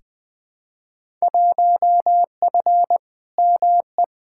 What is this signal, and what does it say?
Morse code, heard as a keyed tone, about 21 wpm, 700 Hz. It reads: 1F ME
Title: Field Day Exchanges Encore - 20wpm